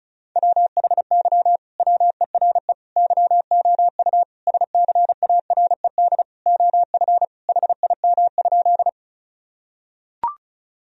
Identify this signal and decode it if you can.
Morse code, heard as a keyed tone, about 34 wpm, 700 Hz. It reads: WHY WERE YOU SCARED OF HIM?
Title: Sentences from Top 1000 Words - Review Part 4 - 35wpm